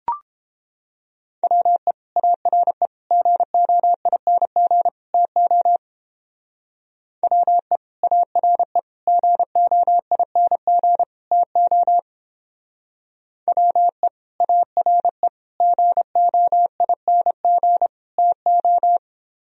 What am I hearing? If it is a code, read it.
Morse code, heard as a keyed tone, about 31 wpm, 700 Hz. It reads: WE ARE GOING TO WE ARE GOING TO WE ARE GOING TO